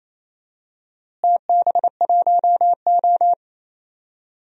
Morse code, heard as a keyed tone, about 28 wpm, 700 Hz. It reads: TB1O
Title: DXCC-100 Call Signs - 4 Characters - Repeated 3x - 28wpm